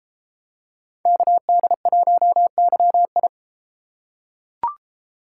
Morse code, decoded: KD1YI

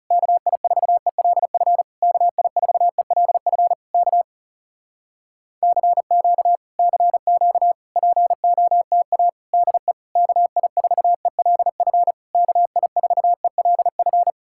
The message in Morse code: KI4ELF KI4ELF K CQ CQ POTA DE KI4ELF KI4ELF

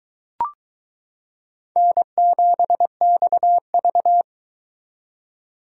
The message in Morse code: N7XV

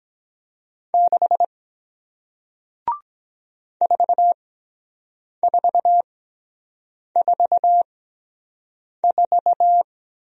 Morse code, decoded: 6 4444

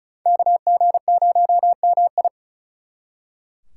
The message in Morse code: KG0MI

